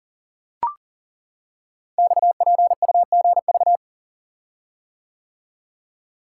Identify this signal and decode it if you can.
Morse code, heard as a keyed tone, about 38 wpm, 690 Hz. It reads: XPUGV
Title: Code Groups - 5 Letters - 40wpm